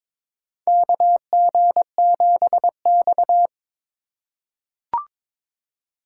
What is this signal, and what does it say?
Morse code, heard as a keyed tone, about 22 wpm, 700 Hz. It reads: KG7X
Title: USA Call Signs - 4 Characters - 8x Character Spacing - 22wpm